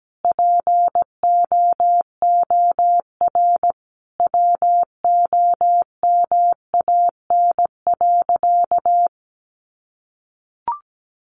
Morse code, heard as a keyed tone, about 17 wpm, 700 Hz. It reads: POOR WOMAN.
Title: Sentences from Top 1100 Words 17wpm